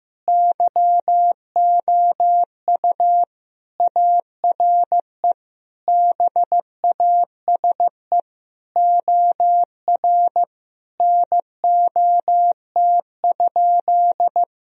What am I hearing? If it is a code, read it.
Morse code, heard as a keyed tone, about 15 wpm, 700 Hz. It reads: YOU ARE BASE OR NOT?
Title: Sentences from Top 1000 Words - Review Part 1 - 15wpm